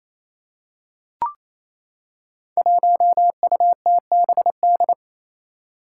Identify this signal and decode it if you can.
Morse code, heard as a keyed tone, about 27 wpm, 700 Hz. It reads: 1UTBD
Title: Five Challenging Letter-Number Combinations 28wpm